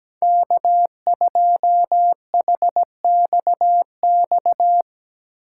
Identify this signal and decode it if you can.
Morse code, heard as a keyed tone, about 17 wpm, 700 Hz. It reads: K2HXX
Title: Speed Racing - USA 1x3 Call Signs 17wpm